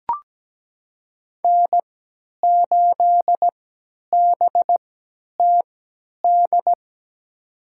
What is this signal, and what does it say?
Morse code, heard as a keyed tone, about 17 wpm, spaced as for 10 wpm, 700 Hz. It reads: N8BTD
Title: USA Call Signs - 3x Character Spacing - 17wpm